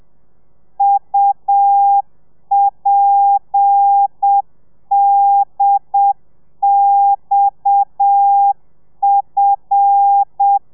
Morse code, heard as a keyed tone, about 7 wpm, 800 Hz. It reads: UPDXF
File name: telegrafia / i2tfj / Lez7 30